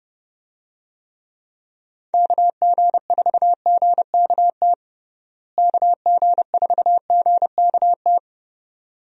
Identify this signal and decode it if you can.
Morse code, heard as a keyed tone, about 30 wpm, 700 Hz. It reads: KG4GKT KG4GKT
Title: USA Call Signs - 6 Characters - Repeated 2x - 30wpm